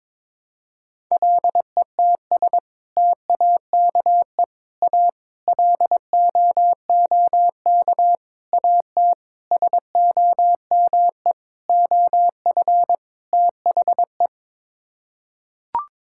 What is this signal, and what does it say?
Morse code, heard as a keyed tone, about 22 wpm, 700 Hz. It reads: LETS TAKE A LOOK AT SOME OF THE
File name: practice / 22wpm / Sets of 8 Words - 22wpm